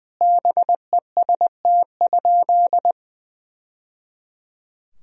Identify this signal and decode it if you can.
Morse code, heard as a keyed tone, about 20 wpm, 700 Hz. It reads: BEST?
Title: Sentences from Top 1000 Words - Review Part 2 - 20wpm